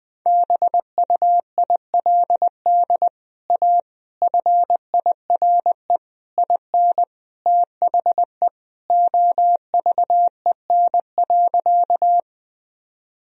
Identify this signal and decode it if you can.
Morse code, heard as a keyed tone, about 20 wpm, 700 Hz. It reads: BUILD A FIRE IN THE OVEN.
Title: Sentences from Top 1000 Words - Review Part 4 - 20wpm